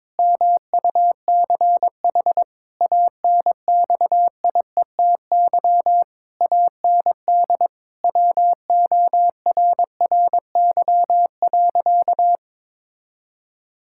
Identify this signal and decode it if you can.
Morse code, heard as a keyed tone, about 22 wpm, 700 Hz. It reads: MUCH ANXIETY AND WORRY.
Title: Pangram Challenge v2 - 22wpm